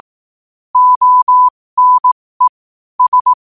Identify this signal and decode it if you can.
Morse code, heard as a keyed tone, about 18 wpm, spaced as for 15 wpm, 1000 Hz. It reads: ONE S
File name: qotdfiles / 15wpm